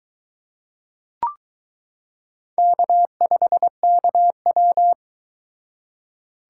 Morse code, heard as a keyed tone, about 22 wpm, 700 Hz. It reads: K5KW